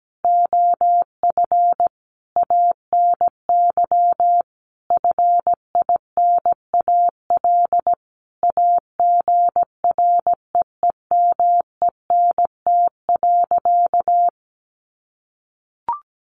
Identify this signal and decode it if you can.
Morse code, heard as a keyed tone, about 17 wpm, 700 Hz. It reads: OF ANY FINAL AGREEMENT.
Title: Sentences from Top 1500 Words 17wpm